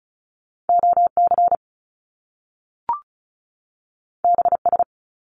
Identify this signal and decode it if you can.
Morse code, heard as a keyed tone, about 36 wpm, 700 Hz. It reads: OC BS
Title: Top Two Letter Combinations in Words 35wpm